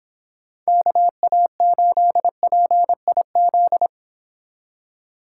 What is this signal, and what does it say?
Morse code, heard as a keyed tone, about 26 wpm, 700 Hz. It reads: KA8PIZ